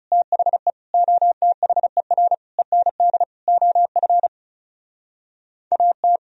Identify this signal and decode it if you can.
Morse code, heard as a keyed tone, about 40 wpm, 700 Hz. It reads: THE OTHER END OF AT